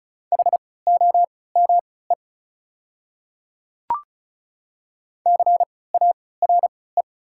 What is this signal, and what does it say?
Morse code, heard as a keyed tone, about 36 wpm, spaced as for 20 wpm, 700 Hz. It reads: HOME CARE
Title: 3x Character Spacing - Top 300 Words - 35wpm